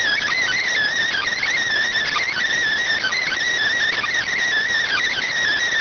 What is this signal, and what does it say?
Emergency Alert System (SAME) tones (header bursts), heard as harsh buzzing bursts. It is heard from 1.1 to 5.3 s.